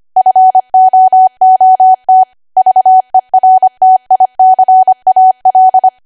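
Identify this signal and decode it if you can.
Morse code, heard as a keyed tone, about 25 wpm, 750 Hz. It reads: FOOT VERTICAL
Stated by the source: ARRL 25 WPM code practice archive.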